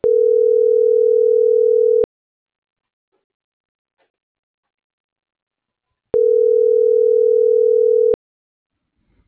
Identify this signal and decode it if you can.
Ringback tone (North America), 440 + 480 Hz.
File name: CENTRAL MABE / out manuel.posadas